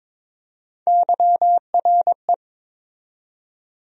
Morse code, heard as a keyed tone, about 21 wpm, 700 Hz. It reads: YRE
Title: USA Call Sign Postfixes - 3 Letters 22wpm